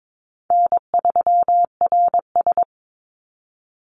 Morse code, heard as a keyed tone, about 22 wpm, 700 Hz. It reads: N3RS
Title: Call Signs Repeated 3x - 22wpm